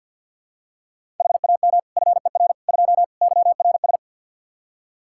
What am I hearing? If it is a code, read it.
Morse code, heard as a keyed tone, about 51 wpm, 700 Hz. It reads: HAM FER 2 YRS
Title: Practice QSO - Exchange 3 - More Info - 50wpm